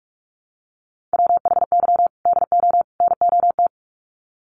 Morse code, heard as a keyed tone, about 44 wpm, 700 Hz. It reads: WHY DO NOT
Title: Sets of 3 Words 45wpm